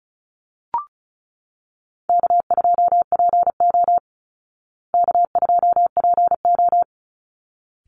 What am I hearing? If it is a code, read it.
Morse code, heard as a keyed tone, about 34 wpm, 700 Hz. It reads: K2PO K2PO
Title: Call Signs Repeated 3x - 35wpm